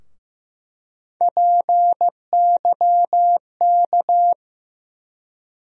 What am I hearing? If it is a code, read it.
Morse code, heard as a keyed tone, about 15 wpm, 700 Hz. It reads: PYK
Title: USA Call Sign Postfixes - 3 Letters 15wpm